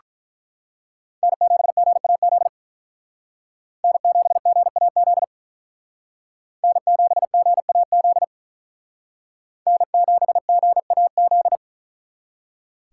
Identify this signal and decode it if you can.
Morse code, heard as a keyed tone, about 42 wpm, 700 Hz. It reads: N7GAZ N7GAZ N7GAZ N7GAZ
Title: Speed Racing - USA Call Signs 35wpm